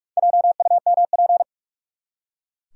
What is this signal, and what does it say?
Morse code, heard as a keyed tone, about 44 wpm, 700 Hz. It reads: JUMP
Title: Four Letter Words - 45wpm